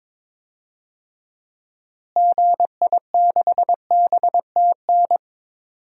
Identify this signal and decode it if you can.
Morse code, heard as a keyed tone, about 22 wpm, 700 Hz. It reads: GI6BTN
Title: DXCC-100 Call Signs - 6 Characters - Repeated 3x - 22wpm